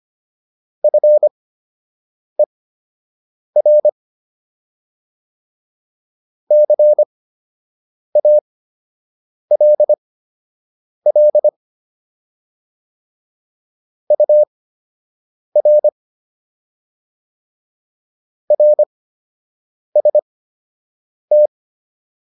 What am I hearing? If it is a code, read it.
Morse code, heard as a keyed tone, about 25 wpm, spaced as for 7 wpm, 600 Hz. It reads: FER CALL UR RST